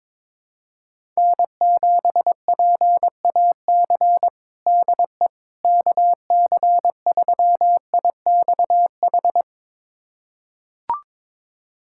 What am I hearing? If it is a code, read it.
Morse code, heard as a keyed tone, about 22 wpm, 700 Hz. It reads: N7PAC DE KC3IXH
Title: USA Callsign DE Callsign 22wpm